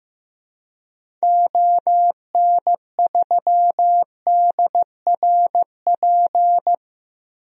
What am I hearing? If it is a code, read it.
Morse code, heard as a keyed tone, about 15 wpm, 700 Hz. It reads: ON3DRP
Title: DXCC-100 Call Signs - 6 Characters - Repeated 2x - 15wpm